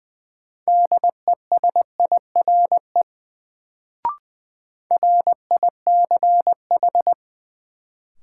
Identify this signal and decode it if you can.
Morse code, heard as a keyed tone, about 20 wpm, 700 Hz. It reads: DESIRE RICH
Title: Sentences from Top 1100 Words - New Word Drill Alt - 20wpm